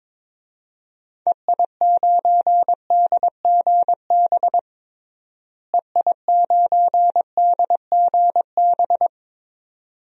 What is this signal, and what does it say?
Morse code, heard as a keyed tone, about 21 wpm, 700 Hz. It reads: EI9DGB EI9DGB
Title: DXCC-100 Call Signs - 6 Characters - Repeated 2x - 22wpm